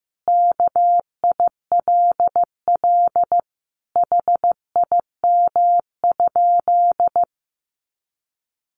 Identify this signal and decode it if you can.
Morse code, heard as a keyed tone, about 15 wpm, 700 Hz. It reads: KILL HIM?
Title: Sentences from Top 1400 Words 15wpm